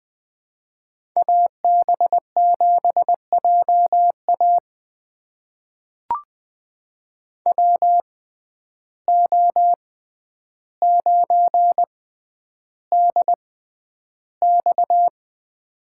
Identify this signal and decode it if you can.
Morse code, heard as a keyed tone, about 21 wpm, spaced as for 10 wpm, 700 Hz. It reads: AB7JA WO9DX
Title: USA Call Signs - 5 Characters - 6x Character Spacing - 20wpm